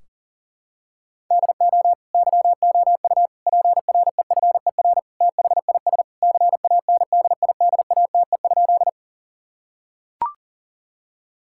Morse code, heard as a keyed tone, about 40 wpm, 700 Hz. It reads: DO YOU PREFER THIS CANDIDATE?
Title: Sentences from Top 1600 Words 40wpm